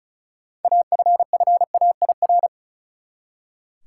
Morse code, heard as a keyed tone, about 36 wpm, 700 Hz. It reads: AFFAIR
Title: Six Letter Words - 35wpm